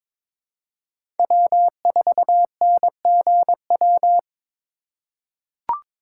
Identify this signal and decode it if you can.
Morse code, heard as a keyed tone, about 22 wpm, 700 Hz. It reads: W4NGW
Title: USA Call Signs - 5 Characters - 2x Character Spacing - 22wpm